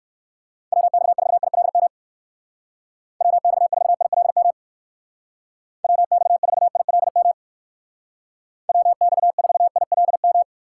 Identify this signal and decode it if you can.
Morse code, heard as a keyed tone, about 59 wpm, 700 Hz. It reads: WX4ILM WX4ILM WX4ILM WX4ILM